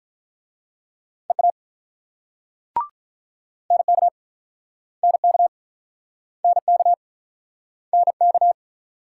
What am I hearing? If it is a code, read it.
Morse code, heard as a keyed tone, about 49 wpm, 700 Hz. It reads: EA NK NK NK NK